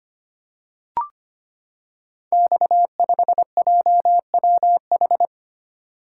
Morse code, heard as a keyed tone, about 25 wpm, 700 Hz. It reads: X5JWH
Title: Code Groups - 5 Letter:Numbers - 25wpm